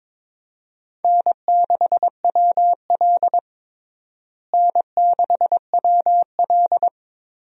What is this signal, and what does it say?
Morse code, heard as a keyed tone, about 22 wpm, 700 Hz. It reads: N6WL N6WL
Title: USA Call Signs - 4 Characters - Repeated 3x - 22wpm